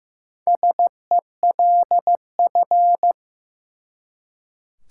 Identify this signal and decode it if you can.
Morse code, heard as a keyed tone, about 15 wpm, 700 Hz. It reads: SELF
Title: Four Letter Words - 15wpm